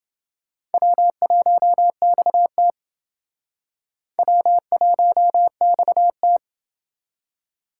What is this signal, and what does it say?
Morse code, heard as a keyed tone, about 29 wpm, 700 Hz. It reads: W1XT W1XT